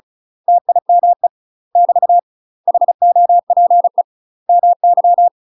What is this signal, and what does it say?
Morse code, heard as a keyed tone, about 35 wpm, 700 Hz. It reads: TIME <BT> HOPE MY
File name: UK MASTER QSOs TRAINING / 35WPM / UK MASTER QSOs 0015